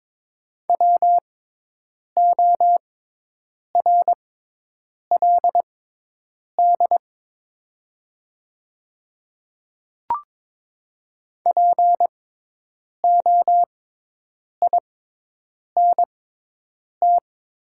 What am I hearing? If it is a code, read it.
Morse code, heard as a keyed tone, about 22 wpm, spaced as for 8 wpm, 700 Hz. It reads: WORLD POINT
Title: 6x Character Spacing - Top 200 Words - 22wpm